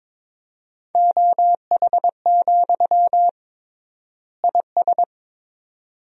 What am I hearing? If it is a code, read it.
Morse code, heard as a keyed tone, about 22 wpm, 700 Hz. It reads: OH, IS
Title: 3x Word-Spacing - Sentences from Top 500 Words - 22wpm